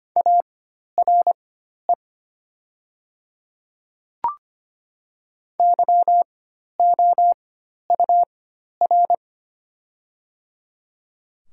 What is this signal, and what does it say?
Morse code, heard as a keyed tone, about 25 wpm, spaced as for 12 wpm, 700 Hz. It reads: ARE YOUR
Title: 4x Character Spacing - Top 100 Words - 25wpm